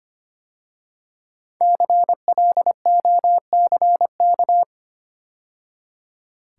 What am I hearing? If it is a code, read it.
Morse code, heard as a keyed tone, about 25 wpm, 700 Hz. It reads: CLOCK
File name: 5 Letter Words - First 3 Letters Given - 25wpm